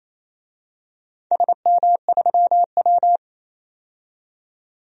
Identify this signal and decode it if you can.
Morse code, heard as a keyed tone, about 28 wpm, 700 Hz. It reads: SM3W
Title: DXCC-100 Call Signs - 4 Characters - 28wpm